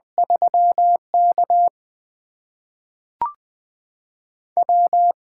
Morse code, heard as a keyed tone, about 20 wpm, spaced as for 5 wpm, 700 Hz. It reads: 3KW
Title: USA Call Signs - 8x Character Spacing - 20wpm